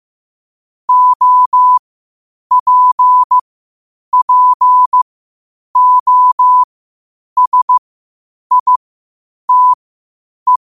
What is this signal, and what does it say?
Morse code, heard as a keyed tone, about 15 wpm, spaced as for 8 wpm, 1000 Hz. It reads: OPPOSITE
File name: qotdfiles / 8wpm